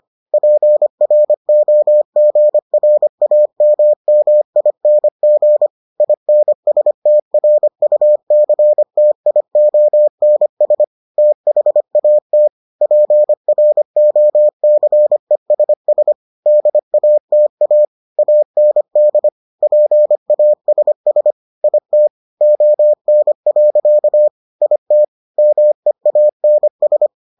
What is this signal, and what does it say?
Morse code, heard as a keyed tone, about 25 wpm, 600 Hz. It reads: PROGRAMMING INSTRUCTIONS THAT PROCESS DATA AND PASS IT ON. IT MEANS